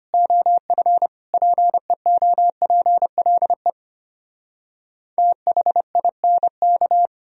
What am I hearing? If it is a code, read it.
Morse code, heard as a keyed tone, about 33 wpm, 700 Hz. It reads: OF PEOPLE THINK